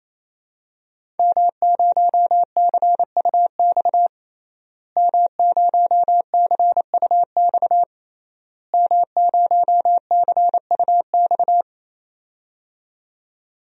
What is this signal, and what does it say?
Morse code, heard as a keyed tone, about 28 wpm, 700 Hz. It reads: M0CUX M0CUX M0CUX
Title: DXCC-100 Call Signs - 5 Characters - Repeated 3x - 28wpm